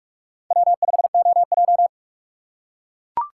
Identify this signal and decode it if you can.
Morse code, heard as a keyed tone, about 44 wpm, 700 Hz. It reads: W5OJ